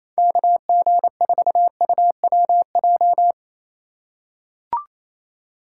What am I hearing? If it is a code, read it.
Morse code, heard as a keyed tone, about 28 wpm, 700 Hz. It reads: KG4UWJ
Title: USA Call Signs - 6 Characters - 4x Character Spacing - 28wpm